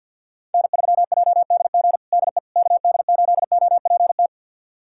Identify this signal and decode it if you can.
Morse code, heard as a keyed tone, about 50 wpm, 700 Hz. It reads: N3JDG DE KD8OPT
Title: USA Callsign DE Callsign 50wpm